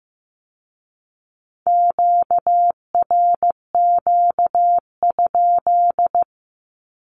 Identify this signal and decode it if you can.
Morse code, heard as a keyed tone, about 15 wpm, 700 Hz. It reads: QRQ?
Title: Q-Codes with Spoken Meaning - 15wpm